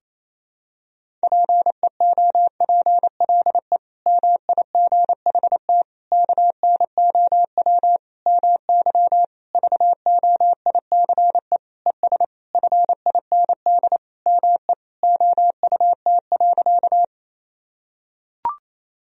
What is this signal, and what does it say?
Morse code, heard as a keyed tone, about 28 wpm, 700 Hz. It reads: PEOPLE MIGHT KNOW MY VOICE ES FIND ME OUT.
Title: Sentences from Top 500 Words and Abbreviations 28wpm